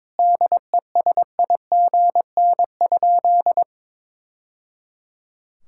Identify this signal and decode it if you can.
Morse code, heard as a keyed tone, about 22 wpm, 700 Hz. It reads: DESIGN?
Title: Sentences from Top 1000 Words - Review Part 5 - 22wpm